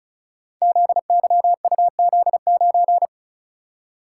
Morse code, heard as a keyed tone, about 36 wpm, 690 Hz. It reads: ZYUZ9
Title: Code Groups - 5 Letter:Numbers - 35wpm